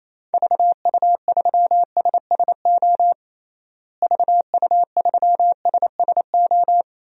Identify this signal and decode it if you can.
Morse code, heard as a keyed tone, about 28 wpm, 700 Hz. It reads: VU3SSO VU3SSO
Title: DXCC-100 Call Signs - 6 Characters - Repeated 2x - 28wpm